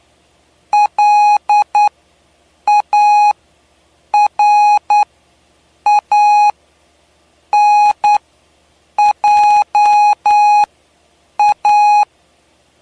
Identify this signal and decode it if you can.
Morse code, heard as a keyed tone, about 9 wpm, spaced as for 6 wpm, 830 Hz. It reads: LARANJA
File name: cw / py2lcd / 07-PY2LCD-Li��o 7